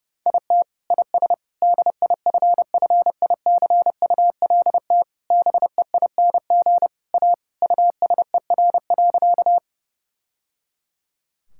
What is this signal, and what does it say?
Morse code, heard as a keyed tone, about 30 wpm, 700 Hz. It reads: IT IS DIFFICULT BEING A USER.
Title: Sentences from Top 1000 Words - Review Part 5 - 30wpm